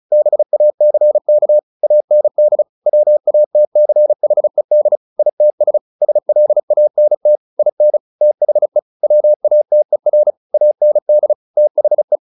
Morse code, heard as a keyed tone, about 34 wpm, 600 Hz. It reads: BACK AND WATCHED ITS SLANT IN THE WATER AND THE
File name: the old man and the sea / the old man and the sea 1 35wpm